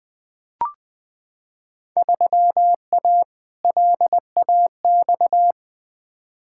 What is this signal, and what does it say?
Morse code, heard as a keyed tone, about 20 wpm, 700 Hz. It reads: 3A LAX
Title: Field Day Exchanges - 20wpm